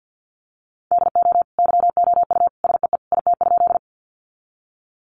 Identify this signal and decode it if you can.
Morse code, heard as a keyed tone, about 50 wpm, 700 Hz. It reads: DO YOU SEE IT?
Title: Sentences from Top 100 Words 50wpm